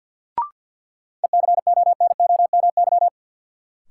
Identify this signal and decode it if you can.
Morse code, heard as a keyed tone, about 52 wpm, 700 Hz. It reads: ECONOMY
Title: Sentences from Top 1000 Words - Review Part 3 - New Word Drill - 50wpm